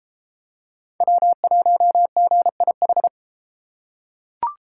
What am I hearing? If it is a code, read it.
Morse code, heard as a keyed tone, about 29 wpm, 700 Hz. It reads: W1GIH